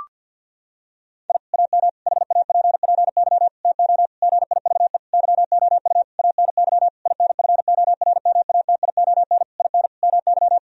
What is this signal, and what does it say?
Morse code, heard as a keyed tone, about 50 wpm, 700 Hz. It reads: I AM HAPPY TO GIVE YOU ANY INFORMATION IN MY
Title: Sentences from Top 1000 Words - Review Part 3 - 50wpm